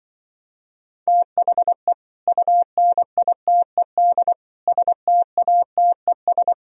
Morse code, heard as a keyed tone, about 24 wpm, 700 Hz. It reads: THE UNITED STATES
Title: Speed-Racing - Sets of 4 Words 20wpm